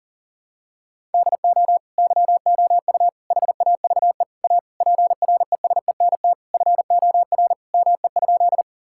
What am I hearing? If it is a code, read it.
Morse code, heard as a keyed tone, about 37 wpm, 700 Hz. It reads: DO YOU HAVE A PRESENT FOR ME?
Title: Sentences from Top 1000 Words - Review Part 5 - 40wpm